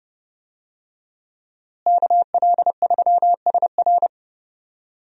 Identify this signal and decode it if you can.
Morse code, heard as a keyed tone, about 30 wpm, 700 Hz. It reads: KL3SR
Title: USA Call Signs - 5 Characters - Repeated 3x - 30wpm